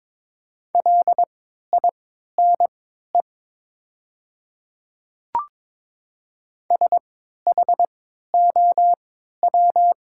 Morse code, heard as a keyed tone, about 21 wpm, spaced as for 12 wpm, 700 Hz. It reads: LINE SHOW
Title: 3x Character Spacing - Top 200 Words - 22wpm